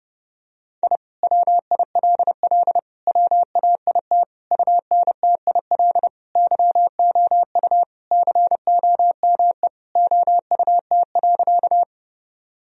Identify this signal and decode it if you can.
Morse code, heard as a keyed tone, about 30 wpm, 700 Hz. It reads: I WILL WAIT UNTIL YOU COME OUT.
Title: Sentences from Top 1000 Words - Review Part 1 - 30wpm